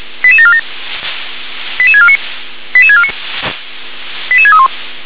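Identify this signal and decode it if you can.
Five-tone selective call (ZVEI), heard as quick runs of five tones, tones of about 70 ms. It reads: 88747 88649 88649 88631
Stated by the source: ZVEI five-tone call 88730 received by SDR.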